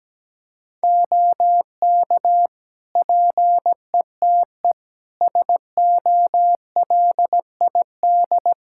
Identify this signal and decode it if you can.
Morse code, heard as a keyed tone, about 17 wpm, 700 Hz. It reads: OK PETE SOLID
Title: Practice QSO - Exchange 3 - More Info - 17wpm